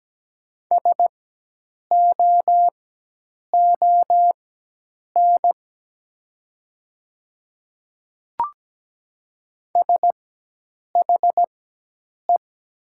Morse code, heard as a keyed tone, about 17 wpm, spaced as for 8 wpm, 700 Hz. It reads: SOON SHE
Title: 4x Character Spacing - Top 500 Words - 17wpm